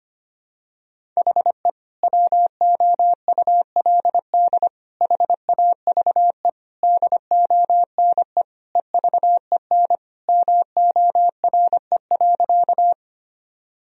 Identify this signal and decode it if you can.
Morse code, heard as a keyed tone, about 25 wpm, 700 Hz. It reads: HE WOULD HAVE DONE EVEN MORE.
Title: Sentences from Top 1000 Words - Review Part 1 - 25wpm